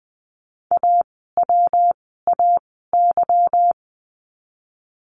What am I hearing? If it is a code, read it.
Morse code, heard as a keyed tone, about 20 wpm, spaced as for 15 wpm, 700 Hz. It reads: AWAY
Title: 146 Lesson 24f - Letter y - Words 20wpm